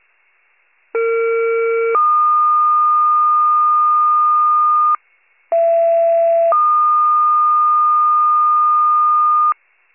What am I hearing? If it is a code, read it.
Two-tone page, heard as a one-second tone, then a three-second tone, about 460 then 1220 Hz, about 670 then 1220 Hz. It is heard from 0.9 to 9.5 s.
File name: pager / 73 RESCUE / wcalert